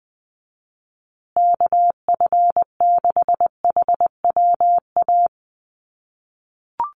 Morse code, heard as a keyed tone, about 19 wpm, 700 Hz. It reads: KF6HWA
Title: USA Call Signs Repeated 2x 20wpm